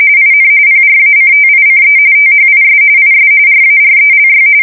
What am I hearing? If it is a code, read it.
RTTY, about 45.45 baud. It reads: IN PRACTICE THIS IS RARELY
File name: fsk170 / printing-telegraph